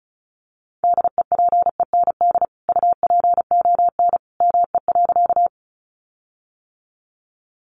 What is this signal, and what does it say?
Morse code, heard as a keyed tone, about 34 wpm, 700 Hz. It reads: DEPEND UPON ME.
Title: Sentences from Top 1300 Words 35wpm